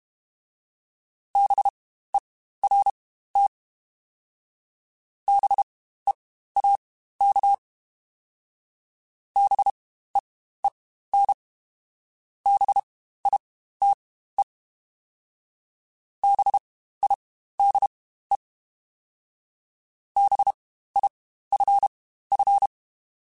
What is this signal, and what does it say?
Morse code, heard as a keyed tone, about 32 wpm, spaced as for 15 wpm, 800 Hz. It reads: BERT BEAK BEEN BITE BIDE BIFF
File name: Spaced Alphas / Spaced Alphas mp3s / Bs sp @30